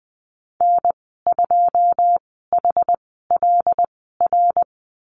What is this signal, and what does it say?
Morse code, heard as a keyed tone, about 20 wpm, spaced as for 14 wpm, 700 Hz. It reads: N2HLR
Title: 122 Lesson 20g - Number 3 - Call Signs 20wpm